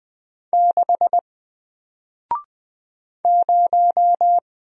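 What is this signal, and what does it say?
Morse code, heard as a keyed tone, about 20 wpm, 700 Hz. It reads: 6 0
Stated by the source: Single letter and number Morse drill at 20 wpm, from N4LCD.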